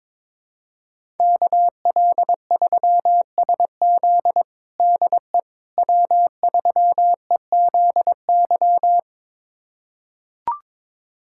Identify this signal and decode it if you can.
Morse code, heard as a keyed tone, about 21 wpm, 700 Hz. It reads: KL3SZ DE W3EZY
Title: USA Callsign DE Callsign - 5 Characters - 22wpm